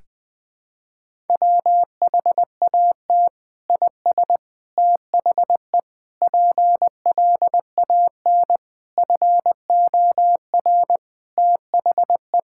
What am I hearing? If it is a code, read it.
Morse code, heard as a keyed tone, about 20 wpm, 700 Hz. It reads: WHAT IS THE PLAN FOR THE
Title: Sentences from Top 1000 Words - Review Part 5 - 20wpm